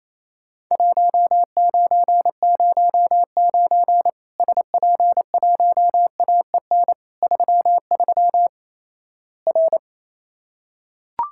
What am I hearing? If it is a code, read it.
Morse code, heard as a keyed tone, about 28 wpm, 700 Hz. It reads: 1909 SP1AEN 33 R
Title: Worked All Europe DX Contest - QTC Exchange - 28wpm